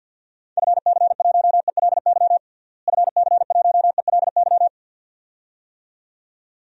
Morse code, heard as a keyed tone, about 52 wpm, 700 Hz. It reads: FC1ELY FC1ELY
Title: DXCC-100 Call Signs - 6 Characters - Repeated 2x - 50wpm